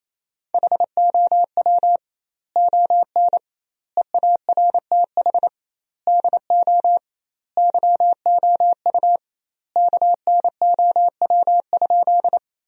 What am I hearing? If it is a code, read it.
Morse code, heard as a keyed tone, about 28 wpm, 700 Hz. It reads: HOW ON EARTH DO YOU KNOW?
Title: 2x Word-Spacing - Sentences from Top 200 Words - 28wpm